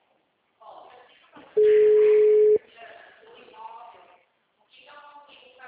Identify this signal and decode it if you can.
Ringback tone (Europe, 425 Hz), with speech.
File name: out- -2004- - - .56